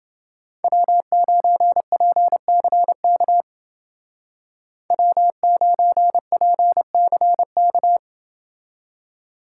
Morse code, heard as a keyed tone, about 28 wpm, 700 Hz. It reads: W9PCK W9PCK